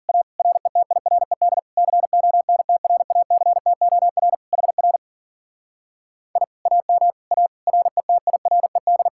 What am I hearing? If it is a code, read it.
Morse code, heard as a keyed tone, about 47 wpm, 700 Hz. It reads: A RETIRED CONTRACTOR HR I AM A RETIRED
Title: Speed-Racing Common QSO Phrases - 40wpm